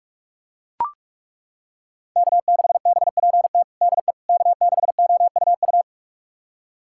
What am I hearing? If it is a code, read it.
Morse code, heard as a keyed tone, about 41 wpm, 700 Hz. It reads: K6BPT DE K6OUU
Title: USA Callsign DE Callsign - 5 Characters - 45wpm